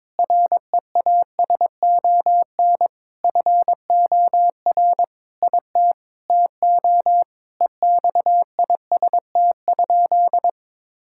Morse code, heard as a keyed tone, about 21 wpm, 700 Hz. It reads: REASON FOR IT TO EXIST?
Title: Sentences from Top 1000 Words - Review Part 5 - 22wpm